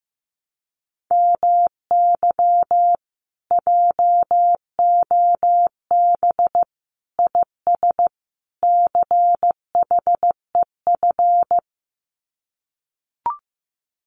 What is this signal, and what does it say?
Morse code, heard as a keyed tone, about 15 wpm, 700 Hz. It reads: MY JOB IS CHEF
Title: Common QSO Phrases Encore 15wpm